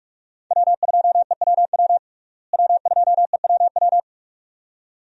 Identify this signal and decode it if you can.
Morse code, heard as a keyed tone, about 46 wpm, 700 Hz. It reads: W2EWW W2EWW
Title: USA Call Signs - 5 Characters - Repeated 2x - 45wpm